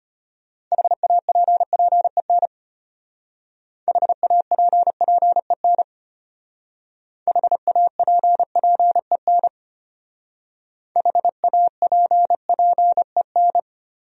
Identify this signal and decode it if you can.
Morse code, heard as a keyed tone, about 33 wpm, 700 Hz. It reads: HAPPEN HAPPEN HAPPEN HAPPEN